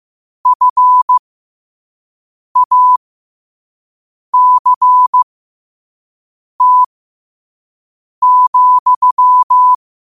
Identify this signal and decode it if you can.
Morse code, heard as a keyed tone, about 15 wpm, spaced as for 5 wpm, 1000 Hz. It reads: FACT,
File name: qotdfiles / 5wpm